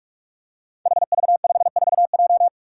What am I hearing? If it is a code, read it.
Morse code, heard as a keyed tone, about 46 wpm, 700 Hz. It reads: HV54J
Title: Five Challenging Letter-Number Combinations 45wpm